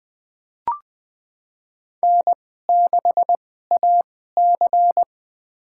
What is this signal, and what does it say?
Morse code, heard as a keyed tone, about 20 wpm, spaced as for 14 wpm, 700 Hz. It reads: N6AC
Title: USA Call Signs - 4 Characters - 2x Character Spacing - 20wpm